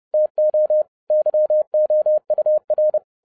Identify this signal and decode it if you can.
Morse code, heard as a keyed tone, about 30 wpm, 610 Hz. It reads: TO YOUR
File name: 2011-10-mp3 / n10-Czech Army Morse Keys 30wpm